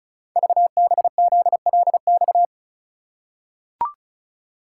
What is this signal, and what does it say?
Morse code, heard as a keyed tone, about 36 wpm, 700 Hz. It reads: VBZLX
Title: Code Groups - 5 Letters - 35wpm